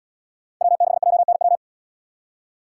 Morse code, heard as a keyed tone, about 77 wpm, 700 Hz. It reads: W6JNW